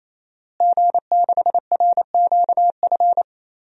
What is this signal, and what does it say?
Morse code, heard as a keyed tone, about 27 wpm, 700 Hz. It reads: G6RQF
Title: DXCC-100 Call Signs - 5 Characters - Repeated 3x - 28wpm